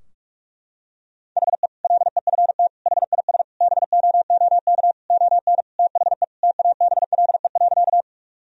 Morse code, heard as a keyed tone, about 47 wpm, 700 Hz. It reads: HE LEFT HIS BOOK ON THE TABLE.
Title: Sentences from Top 1000 Words - Review Part 2 - 45wpm